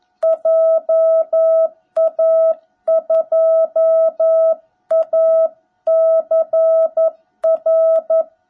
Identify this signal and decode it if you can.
Morse code, heard as a keyed tone, about 11 wpm, 660 Hz. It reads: JA2ACR
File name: jh8cbh / 2021cw / cw17b